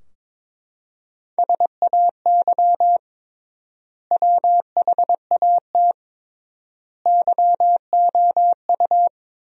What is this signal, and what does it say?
Morse code, heard as a keyed tone, about 22 wpm, 700 Hz. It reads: SAY WHAT YOU
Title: 3x Word-Spacing - Sentences from Top 200 Words - 22wpm